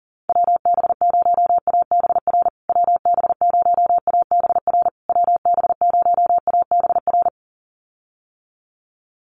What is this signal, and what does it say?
Morse code, heard as a keyed tone, about 37 wpm, 700 Hz. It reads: WB0ABR WB0ABR WB0ABR
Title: USA Call Signs Repeated 3x 40wpm